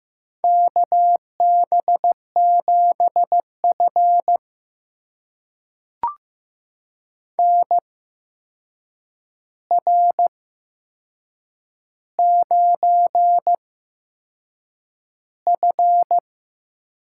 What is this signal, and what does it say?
Morse code, heard as a keyed tone, about 15 wpm, spaced as for 6 wpm, 700 Hz. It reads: KB7F NR9F
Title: USA Call Signs - 4 Characters - 8x Character Spacing - 15wpm